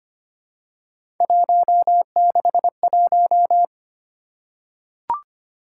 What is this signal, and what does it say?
Morse code, heard as a keyed tone, about 25 wpm, 700 Hz. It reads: 161
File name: Three Letter-Number Combinations - 25wpm